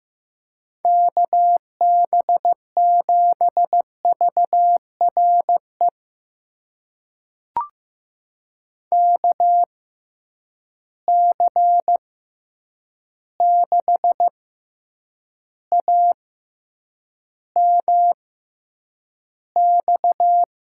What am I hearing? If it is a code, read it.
Morse code, heard as a keyed tone, about 15 wpm, spaced as for 8 wpm, 700 Hz. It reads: KB7VRE KC6AMX
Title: USA Call Signs - 6 Characters - 6x Character Spacing - 15wpm